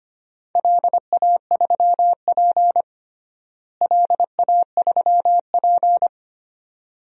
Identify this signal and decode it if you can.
Morse code, heard as a keyed tone, about 25 wpm, 700 Hz. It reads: LA3P LA3P
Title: DXCC-100 Call Signs - 4 Characters - Repeated 2x - 25wpm